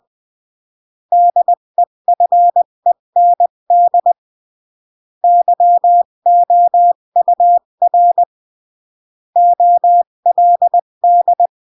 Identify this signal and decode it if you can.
Morse code, heard as a keyed tone, about 20 wpm, 700 Hz. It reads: DEFEND YOUR OLD
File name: CW / jodorowsky / 82 commandments 20-15